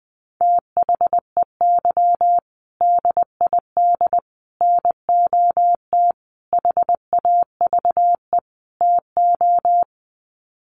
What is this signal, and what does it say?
Morse code, heard as a keyed tone, about 20 wpm, 700 Hz. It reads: THEY DID NOT HAVE TO
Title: Sets of 5 Words Encore2 20wpm